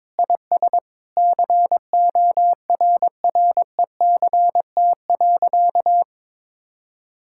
Morse code, heard as a keyed tone, about 22 wpm, 700 Hz. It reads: IS CORRECT.
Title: Sentences from Top 1000 Words - Review Part 5 - 22wpm